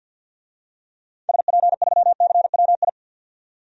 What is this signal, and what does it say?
Morse code, heard as a keyed tone, about 52 wpm, 700 Hz. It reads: SP3CWI
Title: DXCC-100 Call Signs - 6 Characters - 50wpm